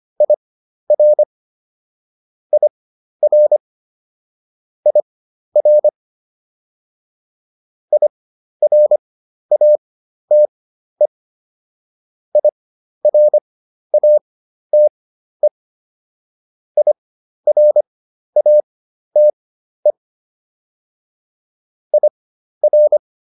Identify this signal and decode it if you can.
Morse code, heard as a keyed tone, about 25 wpm, spaced as for 12 wpm, 600 Hz. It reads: IR IR IR IRATE IRATE IRATE IR